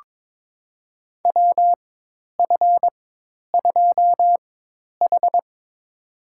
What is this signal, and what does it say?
Morse code, heard as a keyed tone, about 22 wpm, spaced as for 10 wpm, 700 Hz. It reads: WF2H
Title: USA Call Signs - 4 Characters - 4x Character Spacing - 22wpm